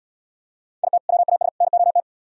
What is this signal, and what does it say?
Morse code, heard as a keyed tone, about 76 wpm, 700 Hz. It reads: IT WAS NOT